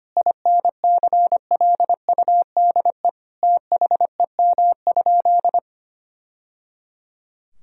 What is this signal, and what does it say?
Morse code, heard as a keyed tone, about 25 wpm, 700 Hz. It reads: INCLUDE THEM?
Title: Sentences from Top 1000 Words - Review Part 3 - 25wpm